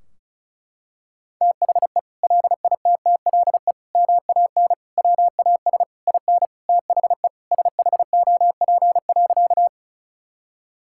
Morse code, heard as a keyed tone, about 34 wpm, 700 Hz. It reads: THE LITTLE MAN WAS IN THE SHOP.
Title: Sentences from Top 1600 Words 35wpm